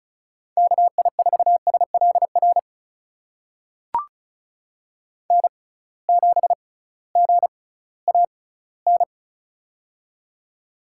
Morse code, heard as a keyed tone, about 36 wpm, spaced as for 19 wpm, 700 Hz. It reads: KI4SLR N7GAN
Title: USA Call Signs - 6x Character Spacing - 35wpm